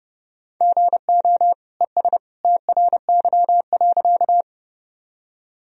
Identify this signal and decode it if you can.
Morse code, heard as a keyed tone, about 30 wpm, 700 Hz. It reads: GO ES TRY.
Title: Sentences from Top 200 Words and Abbreviations 30wpm